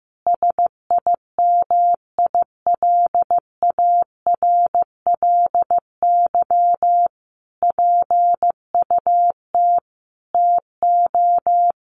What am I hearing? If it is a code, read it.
Morse code, heard as a keyed tone, about 15 wpm, 700 Hz. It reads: SIMILARLY PUT TO
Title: Sentences from Top 1400 Words 15wpm